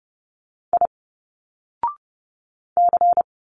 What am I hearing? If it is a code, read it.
Morse code, heard as a keyed tone, about 30 wpm, 700 Hz. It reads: I C
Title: Single Letters 30wpm